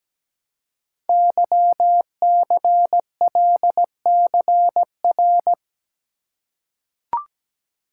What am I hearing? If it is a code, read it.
Morse code, heard as a keyed tone, about 17 wpm, 700 Hz. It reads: YCLCR
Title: Code Groups - 5 Letters - 17wpm